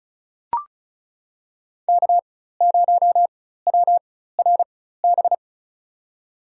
Morse code, heard as a keyed tone, about 36 wpm, spaced as for 16 wpm, 700 Hz. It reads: K0WRB
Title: USA Call Signs - 4x Character Spacing - 35wpm